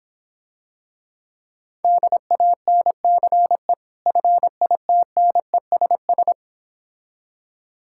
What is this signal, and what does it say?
Morse code, heard as a keyed tone, about 26 wpm, 700 Hz. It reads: DANCE FITNESS